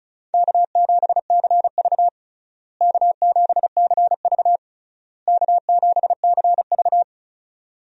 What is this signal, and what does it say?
Morse code, heard as a keyed tone, about 34 wpm, 700 Hz. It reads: K7CV K7CV K7CV
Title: USA Call Signs - 4 Characters - Repeated 3x - 35wpm